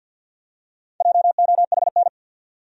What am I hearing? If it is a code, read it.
Morse code, heard as a keyed tone, about 52 wpm, 700 Hz. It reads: JOHN
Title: USA Common Names - Max 5 Letters - 50wpm